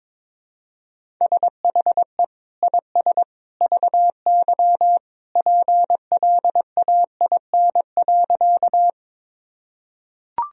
Morse code, heard as a keyed tone, about 22 wpm, 700 Hz. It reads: SHE IS VY PLAIN.
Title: Sentences from Top 400 Words and Abbreviations 22wpm